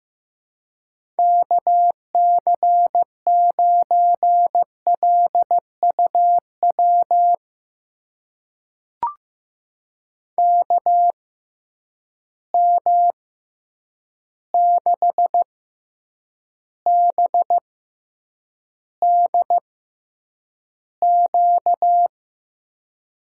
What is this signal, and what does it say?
Morse code, heard as a keyed tone, about 15 wpm, spaced as for 8 wpm, 700 Hz. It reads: KC9LUW KM6BDQ
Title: USA Call Signs - 6 Characters - 6x Character Spacing - 15wpm